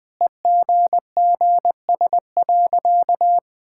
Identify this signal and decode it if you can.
Morse code, heard as a keyed tone, about 21 wpm, 700 Hz. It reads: EGGS.
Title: Easy Sentences with Hints - 20wpm